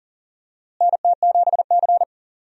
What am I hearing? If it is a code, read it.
Morse code, heard as a keyed tone, about 40 wpm, 700 Hz. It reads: NT7C
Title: USA Call Signs - 4 Characters - Repeated 3x - 40wpm